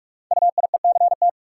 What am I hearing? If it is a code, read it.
Morse code, heard as a keyed tone, about 44 wpm, 690 Hz. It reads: UIECT
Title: Code Groups - 5 Letters - 45wpm